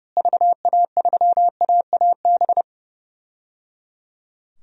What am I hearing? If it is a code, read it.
Morse code, heard as a keyed tone, about 30 wpm, 700 Hz. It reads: VA3AAB
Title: DXCC-100 Call Signs - 6 Characters - Repeated 3x - 30wpm